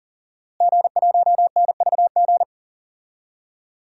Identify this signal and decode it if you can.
Morse code, heard as a keyed tone, about 40 wpm, 700 Hz. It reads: G1NVG
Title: DXCC-100 Call Signs - 5 Characters - Repeated 2x - 40wpm